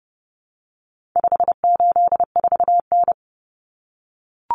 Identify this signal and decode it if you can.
Morse code, heard as a keyed tone, about 30 wpm, 700 Hz. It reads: 584N
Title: Common Cut Numbers - 4-Digit 30wpm